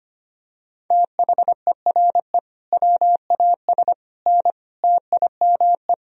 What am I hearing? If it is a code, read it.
Morse code, heard as a keyed tone, about 25 wpm, 700 Hz. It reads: THERE WAS N TIME
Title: Sentences from Top 400 Words and Abbreviations 25wpm